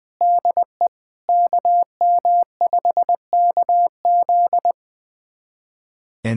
Morse code, heard as a keyed tone, about 21 wpm, 700 Hz. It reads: DE KM5KZ
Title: USA Callsign DE Callsign - 5 Characters - 20wpm